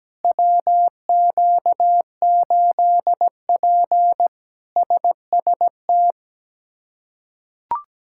Morse code, heard as a keyed tone, about 17 wpm, 700 Hz. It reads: WQ8P SST
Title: K1USN SST QSO Practice - 17wpm